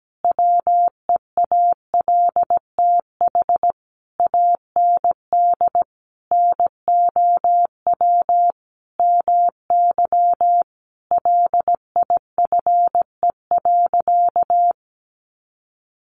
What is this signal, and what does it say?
Morse code, heard as a keyed tone, about 17 wpm, 700 Hz. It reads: WEALTH AND NOW MY LIFE.
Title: Sentences from Top 1200 Words 17wpm